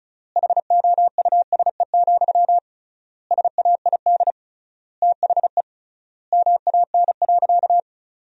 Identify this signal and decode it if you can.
Morse code, heard as a keyed tone, about 36 wpm, 700 Hz. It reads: HOUSE, SAID THE MAN.
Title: 3x Word-Spacing - Sentences from Top 400 Words - 35wpm